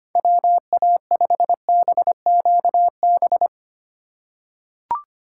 Morse code, heard as a keyed tone, about 25 wpm, 700 Hz. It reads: WA5BQB
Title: USA Call Signs - 6 Characters - 6x Character Spacing - 25wpm